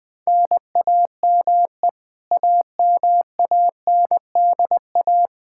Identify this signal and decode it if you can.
Morse code, heard as a keyed tone, about 21 wpm, 700 Hz. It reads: NAME AMANDA
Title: K1USN SST QSO Practice 20wpm